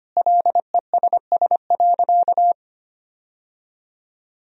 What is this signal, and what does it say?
Morse code, heard as a keyed tone, about 25 wpm, 700 Hz. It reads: LESS.
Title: Sentences from Top 1000 Words - Review Part 2 - 25wpm